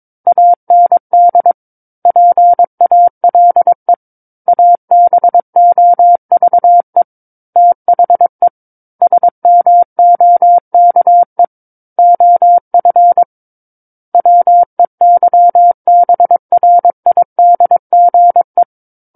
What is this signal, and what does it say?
Morse code, heard as a keyed tone, about 22 wpm, 700 Hz. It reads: AND PALE ABOVE THE SMOKE OF WEYBRIDGE
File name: Book One / 13-Chapter 13 - 22 WPM